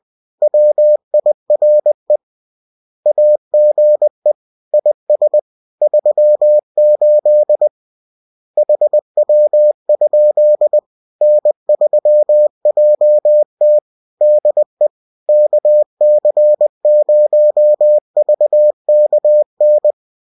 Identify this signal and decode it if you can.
Morse code, heard as a keyed tone, about 20 wpm, 600 Hz. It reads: WIRE AGE IS 38 HW? N3JT DE KC0VKN